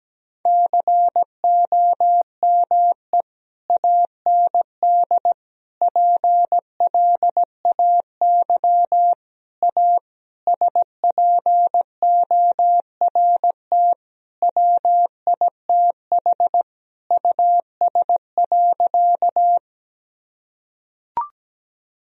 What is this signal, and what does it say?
Morse code, heard as a keyed tone, about 17 wpm, 700 Hz. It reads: COME AND PLAY A SPORT WITH US.
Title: Sentences from Top 1000 Words - Review Part 4 - 17wpm